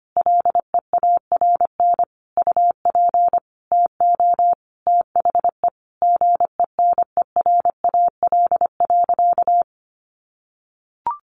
Morse code, heard as a keyed tone, about 25 wpm, 700 Hz. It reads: LEARN UP TO THE GENERAL.
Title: Sentences from Top 1500 Words 25wpm